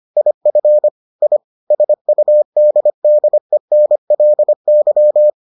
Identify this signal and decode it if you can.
Morse code, heard as a keyed tone, about 25 wpm, 600 Hz. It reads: IF I SUDDENLY